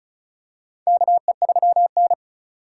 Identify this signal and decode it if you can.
Morse code, heard as a keyed tone, about 35 wpm, 700 Hz. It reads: KE3N
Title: USA Call Signs - 4 Characters - 2x Character Spacing - 35wpm